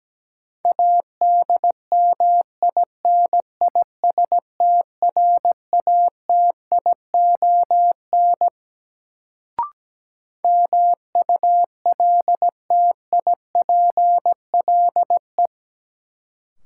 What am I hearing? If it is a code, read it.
Morse code, heard as a keyed tone, about 17 wpm, 700 Hz. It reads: ADMINISTRATION MULTIPLE
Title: Sets of 5 Words - New Word Drill - 17wpm